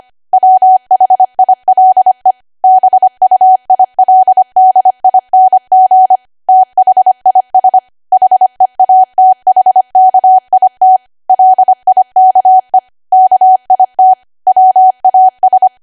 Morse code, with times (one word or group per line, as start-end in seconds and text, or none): 0.3-2.3 WHILE
2.6-6.2 BUILDING
6.5-7.8 THIS
8.1-11.0 HEATHKIT
11.3-12.8 LIKE
13.1-14.1 KIT
14.5-15.7 WAS